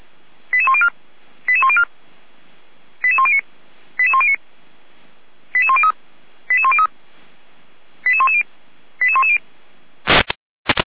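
Five-tone selective call (ZVEI), heard as quick runs of five tones, tones of about 70 ms. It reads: 88104 88104 88109 88109 88103 88103 88110 88110
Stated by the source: ZVEI five-tone call received on an SDR by feuerwehr-orlamuende.org, fire brigade Orlamünde.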